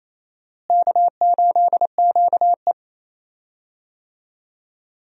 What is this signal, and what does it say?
Morse code, heard as a keyed tone, about 27 wpm, 700 Hz. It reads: K8QE
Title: USA Call Signs - 4 Characters - Repeated 2x - 28wpm